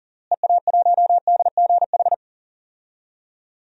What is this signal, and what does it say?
Morse code, heard as a keyed tone, about 37 wpm, 700 Hz. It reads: EA1DGH
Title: DXCC-100 Call Signs - 6 Characters - Repeated 3x - 40wpm